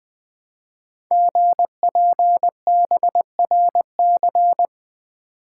Morse code, heard as a keyed tone, about 20 wpm, 700 Hz. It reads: GPBRC
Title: Code Groups - 5 Letters - 20wpm